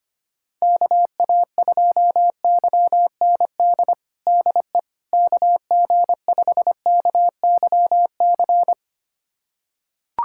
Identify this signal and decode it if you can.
Morse code, heard as a keyed tone, about 25 wpm, 700 Hz. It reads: KA2YND DE KG5KYC
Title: USA Callsign DE Callsign - 6 Characters - 25wpm